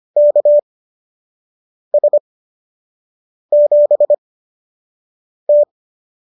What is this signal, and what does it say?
Morse code, heard as a keyed tone, about 25 wpm, spaced as for 6 wpm, 590 Hz. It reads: KS7T